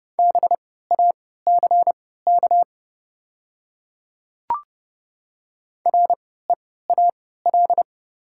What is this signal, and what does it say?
Morse code, heard as a keyed tone, about 30 wpm, spaced as for 17 wpm, 700 Hz. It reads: BACK REAL
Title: 3x Character Spacing - Top 300 Words - 30wpm